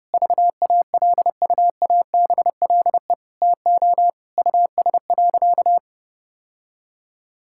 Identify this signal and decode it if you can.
Morse code, heard as a keyed tone, about 32 wpm, 700 Hz. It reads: VALUABLE TO US.
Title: Sentences from Top 1600 Words 30wpm